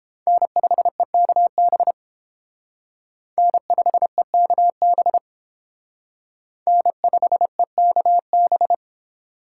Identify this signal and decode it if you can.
Morse code, heard as a keyed tone, about 29 wpm, 700 Hz. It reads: N5EKB N5EKB N5EKB